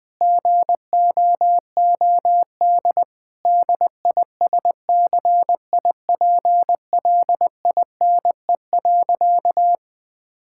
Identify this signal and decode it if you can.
Morse code, heard as a keyed tone, about 20 wpm, 700 Hz. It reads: GOOD DISCIPLINE.
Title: Sentences from Top 1000 Words - Review Part 6 - 20wpm